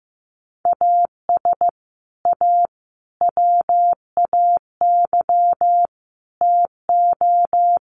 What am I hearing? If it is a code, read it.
Morse code, heard as a keyed tone, about 15 wpm, 700 Hz. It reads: AS A WAY TO
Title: Sets of 4 Words 15wpm